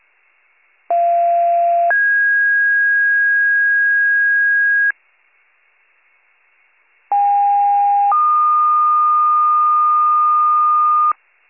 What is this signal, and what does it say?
Two-tone page, heard as a one-second tone, then a three-second tone, about 690 then 1650 Hz, about 810 then 1220 Hz. It is heard from 0.9 to 11.1 s.